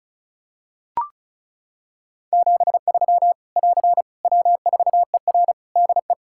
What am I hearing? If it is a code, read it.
Morse code, heard as a keyed tone, about 34 wpm, 700 Hz. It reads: 73 <AR> W4ER DE
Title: K1USN SST QSO Practice 35wpm